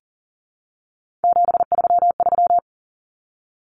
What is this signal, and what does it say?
Morse code, heard as a keyed tone, about 40 wpm, 700 Hz. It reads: 733
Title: Common Cut Numbers - 3-Digit 40wpm